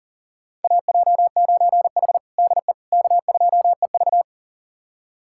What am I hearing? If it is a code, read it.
Morse code, heard as a keyed tone, about 40 wpm, 700 Hz. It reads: AJ9H DE K2EV
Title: USA Callsign DE Callsign - 4 Characters - 40wpm